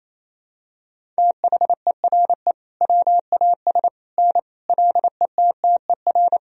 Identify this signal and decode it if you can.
Morse code, heard as a keyed tone, about 28 wpm, 700 Hz. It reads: THERE WAS N LETTER
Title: Sentences from Top 300 Words and Abbreviations 28wpm